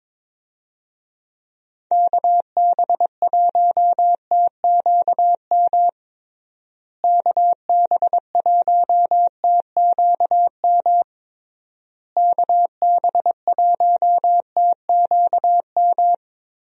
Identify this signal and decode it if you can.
Morse code, heard as a keyed tone, about 22 wpm, 700 Hz. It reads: KB1TQM KB1TQM KB1TQM
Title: USA Call Signs - 6 Characters - Repeated 3x - 22wpm